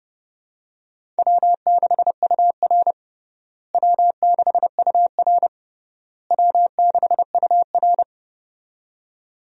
Morse code, heard as a keyed tone, about 30 wpm, 700 Hz. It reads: W6UR W6UR W6UR
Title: USA Call Signs - 4 Characters - Repeated 3x - 30wpm